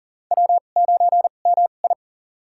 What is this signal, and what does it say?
Morse code, heard as a keyed tone, about 41 wpm, spaced as for 29 wpm, 690 Hz. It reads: W9MI
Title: USA Call Signs - 4 Characters - 2x Character Spacing - 40wpm